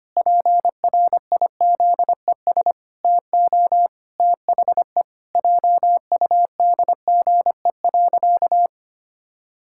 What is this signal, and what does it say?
Morse code, heard as a keyed tone, about 25 wpm, 700 Hz. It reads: PRIZES TO THE JUDGE.
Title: Pangram Challenge v2 - 25wpm